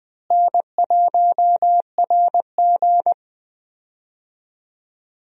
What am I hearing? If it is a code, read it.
Morse code, heard as a keyed tone, about 20 wpm, 700 Hz. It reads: N1RG
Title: USA Call Signs - 4 Characters - Repeated 2x - 20wpm